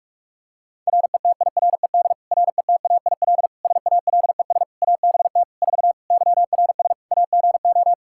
Morse code, heard as a keyed tone, about 46 wpm, 700 Hz. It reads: RETIRED RETAIL SALES ABT 4 YRS AGO
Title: Practice QSO - Exchange 3 - More Info - 45wpm